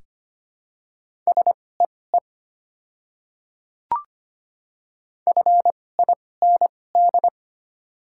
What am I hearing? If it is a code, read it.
Morse code, heard as a keyed tone, about 25 wpm, spaced as for 18 wpm, 690 Hz. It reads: SEE FIND
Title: 2x Character Spacing - Top 100 Words - 25wpm